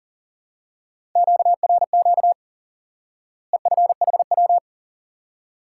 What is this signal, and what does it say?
Morse code, heard as a keyed tone, about 40 wpm, 700 Hz. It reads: QRQ EFHW
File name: IQE Course 3 - Lesson 89 - YAGI - Two Elements - 40wpm